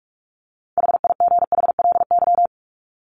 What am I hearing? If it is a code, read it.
Morse code, heard as a keyed tone, about 44 wpm, 700 Hz. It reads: HIGHLY
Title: LY Suffix - Top 100 Words with Base Words 45wpm